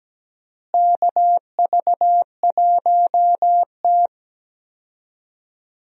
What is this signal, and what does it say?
Morse code, heard as a keyed tone, about 17 wpm, 700 Hz. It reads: KV1T
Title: USA Call Signs - 4 Characters - Repeated 3x - 17wpm